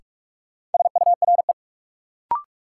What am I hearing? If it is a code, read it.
Morse code, heard as a keyed tone, about 46 wpm, 700 Hz. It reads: SURE
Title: Sets of 3 Words - New Word Drill - 45wpm